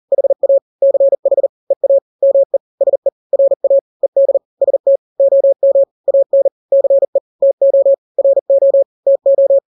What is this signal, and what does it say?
Morse code, heard as a keyed tone, about 40 wpm, 550 Hz. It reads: HA CH EA ME SE RA ED ST OM AN CE TO RO TO
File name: n-grams / 2-grams 40 35 WPM